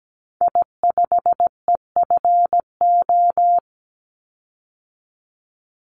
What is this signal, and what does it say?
Morse code, heard as a keyed tone, about 17 wpm, 700 Hz. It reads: I5EFO
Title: Call Signs Repeated 3x - 17wpm